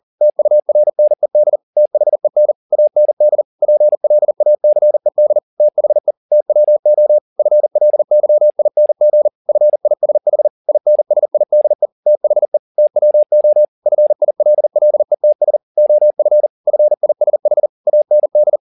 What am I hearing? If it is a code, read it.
Morse code, heard as a keyed tone, about 40 wpm, 600 Hz. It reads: TURNED THEN AND PLACED THE TWO FLYING FISH INSIDE THE TWO FILLETS OF FISH AND
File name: the old man and the sea 2 40wpm